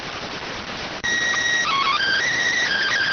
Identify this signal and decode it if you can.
SSTV (slow-scan television), PD 290.